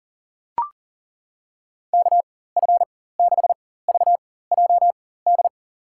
Morse code, heard as a keyed tone, about 40 wpm, spaced as for 19 wpm, 700 Hz. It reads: KF6VJD
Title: USA Call Signs - 6 Characters - 4x Character Spacing - 40wpm